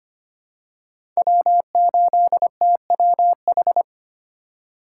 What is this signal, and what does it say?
Morse code, heard as a keyed tone, about 25 wpm, 700 Hz. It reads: W8TWH
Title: USA Call Signs - 5 Characters - 3x Character Spacing - 25wpm